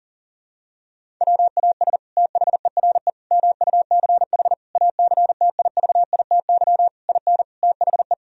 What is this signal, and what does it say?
Morse code, heard as a keyed tone, about 40 wpm, 700 Hz. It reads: WAS THERE MUCH ACTIVITY IN THE
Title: Sentences from Top 1000 Words - Review Part 4 - 40wpm